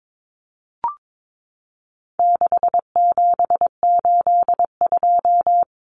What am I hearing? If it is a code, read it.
Morse code, heard as a keyed tone, about 22 wpm, 700 Hz. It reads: 6782
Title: Common Cut Numbers - 4-Digit 22wpm